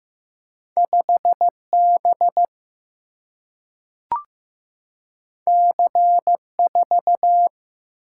Morse code, heard as a keyed tone, about 15 wpm, 700 Hz. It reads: 5 B C 4
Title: Single-Letter Single-Number 15wpm